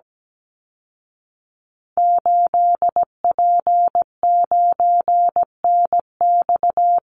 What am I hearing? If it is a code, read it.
Morse code, heard as a keyed tone, about 17 wpm, 700 Hz. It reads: 8P9NX
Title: Call Signs Repeated 2x - 17wpm